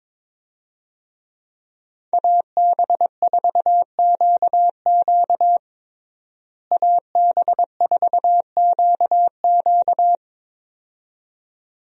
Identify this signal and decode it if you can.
Morse code, heard as a keyed tone, about 22 wpm, 700 Hz. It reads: AB4QQ AB4QQ
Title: USA Call Signs - 5 Characters - Repeated 2x - 22wpm